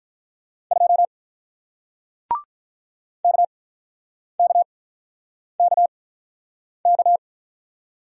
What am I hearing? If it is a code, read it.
Morse code, heard as a keyed tone, about 48 wpm, spaced as for 9 wpm, 700 Hz. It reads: 2 KKKK